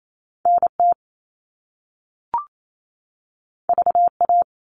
Morse code, heard as a keyed tone, about 28 wpm, 700 Hz. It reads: NT VA
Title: All Cut Numbers - 2-Digit 28wpm